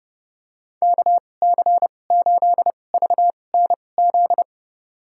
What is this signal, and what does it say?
Morse code, heard as a keyed tone, about 30 wpm, spaced as for 22 wpm, 700 Hz. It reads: KC8VNZ
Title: USA Call Signs - 6 Characters - 2x Character Spacing - 30wpm